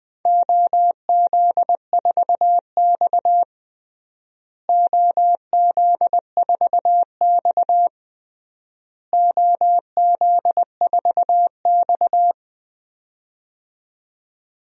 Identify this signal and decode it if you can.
Morse code, heard as a keyed tone, about 20 wpm, 700 Hz. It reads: OZ4X OZ4X OZ4X
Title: DXCC-100 Call Signs - 4 Characters - Repeated 3x - 20wpm